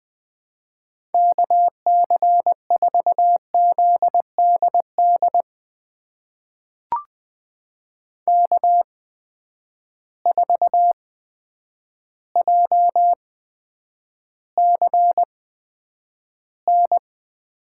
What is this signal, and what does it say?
Morse code, heard as a keyed tone, about 20 wpm, spaced as for 9 wpm, 700 Hz. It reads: KC4ZDD K4JCN
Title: USA Call Signs - 8x Character Spacing - 20wpm